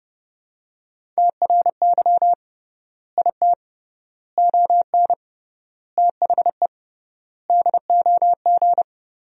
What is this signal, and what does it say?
Morse code, heard as a keyed tone, about 30 wpm, 700 Hz. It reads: TRY IT ON THE DOG
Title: 3x Word-Spacing - Sentences from Top 400 Words - 30wpm